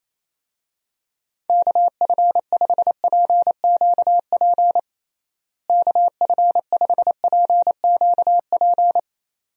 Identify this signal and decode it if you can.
Morse code, heard as a keyed tone, about 28 wpm, 700 Hz. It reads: KF5PQP KF5PQP
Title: USA Call Signs - 6 Characters - Repeated 2x - 28wpm